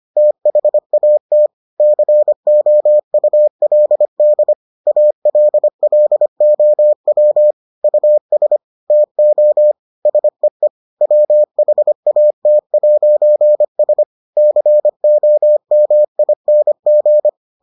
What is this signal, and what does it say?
Morse code, heard as a keyed tone, about 25 wpm, 600 Hz. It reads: THAT COULD ALLOW US TO SEE WHAT'S COMING